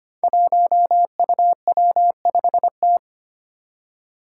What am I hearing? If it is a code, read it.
Morse code, heard as a keyed tone, about 25 wpm, 700 Hz. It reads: 1UW5T
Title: Five Challenging Letter-Number Combinations 25wpm